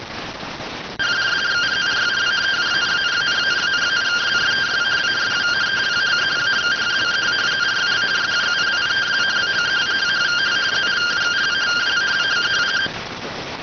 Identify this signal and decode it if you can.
Two-tone teleprinter signal, about 170 Hz shift.